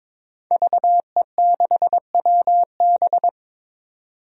Morse code, heard as a keyed tone, about 22 wpm, 700 Hz. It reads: VE6WB
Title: DXCC-100 Call Signs - 5 Characters - 22wpm